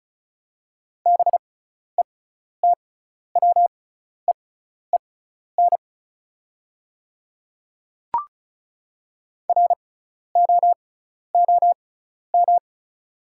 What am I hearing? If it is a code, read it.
Morse code, heard as a keyed tone, about 34 wpm, spaced as for 12 wpm, 700 Hz. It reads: BETWEEN ROOM
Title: 6x Character Spacing - Top 300 Words - 35wpm